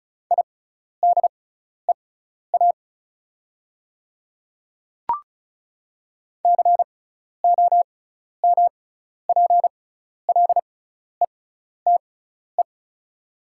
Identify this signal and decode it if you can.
Morse code, heard as a keyed tone, about 36 wpm, spaced as for 12 wpm, 690 Hz. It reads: IDEA COMPLETE
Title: 6x Character Spacing - Top 500 Words - 35wpm